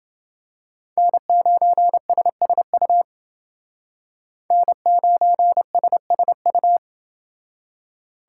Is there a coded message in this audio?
Yes: Morse code from 1.0 to 6.8 s, heard as a keyed tone, about 28 wpm, 700 Hz. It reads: N9SSU N9SSU